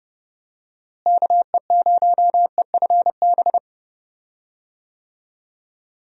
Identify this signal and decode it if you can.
Morse code, heard as a keyed tone, about 30 wpm, 700 Hz. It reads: KE0EFB
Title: USA Call Signs - 6 Characters - Repeated 3x - 30wpm